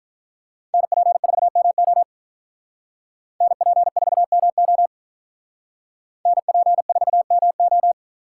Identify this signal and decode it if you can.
Morse code, heard as a keyed tone, about 46 wpm, 700 Hz. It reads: NP4MO NP4MO NP4MO